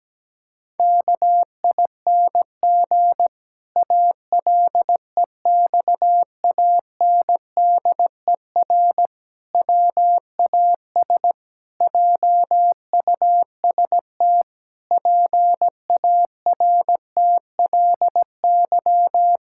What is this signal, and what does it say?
Morse code, heard as a keyed tone, about 17 wpm, 700 Hz. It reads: KING ALEXANDER WAS JUST PARTLY
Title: Pangram Challenge v2 - 17wpm